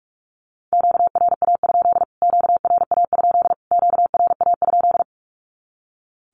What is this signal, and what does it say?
Morse code, heard as a keyed tone, about 45 wpm, 700 Hz. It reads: QRA? QRA? QRA?
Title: Q-Codes Repeated 3x 45wpm